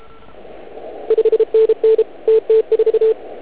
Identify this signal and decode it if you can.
Morse code, heard as a keyed tone, about 31 wpm, 440 Hz. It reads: 5NN TT4